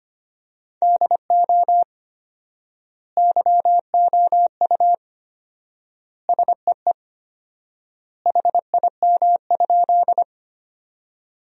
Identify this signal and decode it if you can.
Morse code, heard as a keyed tone, about 25 wpm, 700 Hz. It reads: DO YOU SEE HIM?
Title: 4x Word-Spacing - Sentences from Top 100 Words - 25wpm